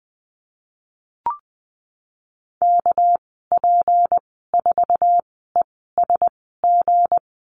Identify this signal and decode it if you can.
Morse code, heard as a keyed tone, about 19 wpm, spaced as for 14 wpm, 700 Hz. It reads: KP4ESG
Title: 200 Lesson 32g - Letter k - Call Signs 20wpm